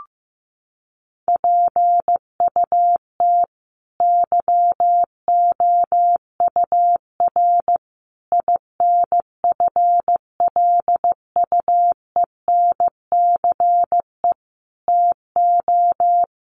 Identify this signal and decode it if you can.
Morse code, heard as a keyed tone, about 15 wpm, 700 Hz. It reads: PUT YOUR INFLUENCE TO
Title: Sentences from Top 1100 Words 15wpm